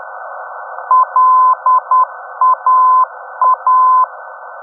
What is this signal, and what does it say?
Morse code, heard as a keyed tone, about 10 wpm, 1020 Hz. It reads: LAA
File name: europe / d-5078- dec 03 - ndb laa-ti